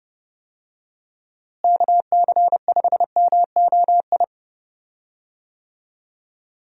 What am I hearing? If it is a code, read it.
Morse code, heard as a keyed tone, about 31 wpm, 700 Hz. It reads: KC5MOI
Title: USA Call Signs - 6 Characters - Repeated 3x - 30wpm